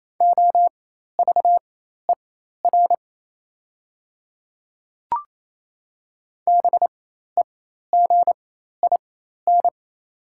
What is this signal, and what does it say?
Morse code, heard as a keyed tone, about 28 wpm, spaced as for 13 wpm, 700 Hz. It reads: OVER BEGIN
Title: 4x Character Spacing - Top 300 Words - 28wpm